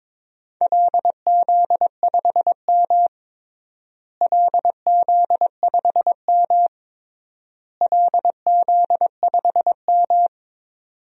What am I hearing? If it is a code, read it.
Morse code, heard as a keyed tone, about 22 wpm, 700 Hz. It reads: LZ5M LZ5M LZ5M
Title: DXCC-100 Call Signs - 4 Characters - Repeated 3x - 22wpm